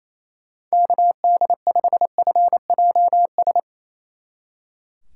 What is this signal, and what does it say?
Morse code, heard as a keyed tone, about 35 wpm, 700 Hz. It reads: KD5FJS
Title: Speed Racing - USA 2x3 Call Signs 28wpm